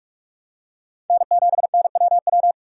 Morse code, heard as a keyed tone, about 46 wpm, 700 Hz. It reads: N7NWW
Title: USA Call Signs - 5 Characters - 6x Character Spacing - 45wpm